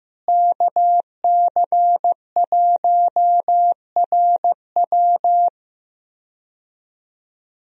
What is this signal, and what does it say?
Morse code, heard as a keyed tone, about 15 wpm, 700 Hz. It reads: KC1RW
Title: USA Call Signs - 5 Characters - Repeated 2x - 15wpm